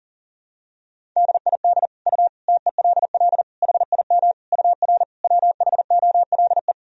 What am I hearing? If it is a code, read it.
Morse code, heard as a keyed tone, about 37 wpm, 700 Hz. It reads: DID U TELL HIM UR WHOLE
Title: Sentences from Top 400 Words and Abbreviations 40wpm